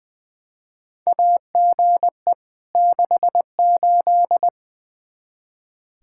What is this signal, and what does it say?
Morse code, heard as a keyed tone, about 20 wpm, 700 Hz. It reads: AGE 68
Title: Practice QSO - Exchange 3 - More Info - 20wpm